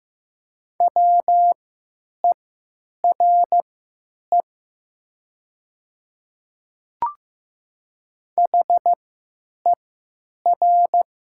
Morse code, heard as a keyed tone, about 15 wpm, spaced as for 9 wpm, 690 Hz. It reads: WERE HER
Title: 3x Character Spacing - Top 100 Words - 15wpm